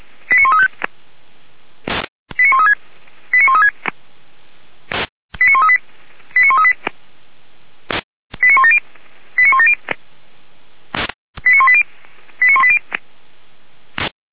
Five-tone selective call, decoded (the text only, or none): 89135 89136 89136 89138 89138 89170 89170 89180 89180